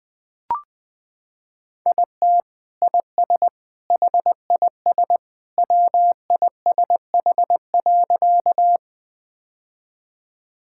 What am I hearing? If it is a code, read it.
Morse code, heard as a keyed tone, about 20 wpm, 700 Hz. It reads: IT IS HIS WISH.
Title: Sentences from Top 1600 Words 20wpm